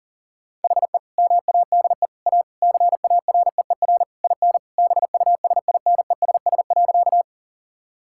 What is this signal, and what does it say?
Morse code, heard as a keyed tone, about 40 wpm, 700 Hz. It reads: HE MADE A CAREER IN BUSINESS.
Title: Sentences from Top 1000 Words - Review Part 6 - 40wpm